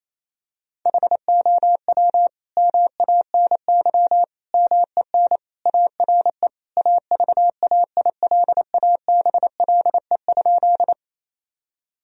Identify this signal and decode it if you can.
Morse code, heard as a keyed tone, about 28 wpm, 700 Hz. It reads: HOW MANY MEN ARE AVAILABLE?
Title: Sentences from Top 1000 Words - Review Part 3 - 28wpm